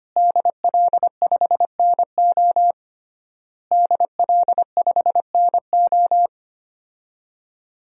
Morse code, heard as a keyed tone, about 25 wpm, 700 Hz. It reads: DL5NO DL5NO
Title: DXCC-100 Call Signs - 5 Characters - Repeated 3x - 25wpm